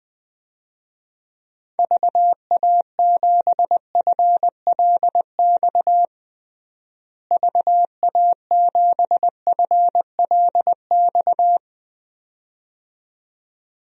Morse code, heard as a keyed tone, about 20 wpm, 700 Hz. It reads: VA7FLX VA7FLX
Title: DXCC-100 Call Signs - 6 Characters - Repeated 2x - 20wpm